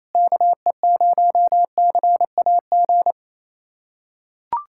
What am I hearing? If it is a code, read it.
Morse code, heard as a keyed tone, about 27 wpm, 700 Hz. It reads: KE0CAG
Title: USA Callsign DE Callsign 28wpm